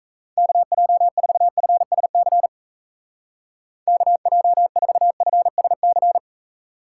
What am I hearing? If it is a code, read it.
Morse code, heard as a keyed tone, about 34 wpm, 700 Hz. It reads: KJ4FSC KJ4FSC
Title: Speed Racing - USA 2x3 Call Signs 28wpm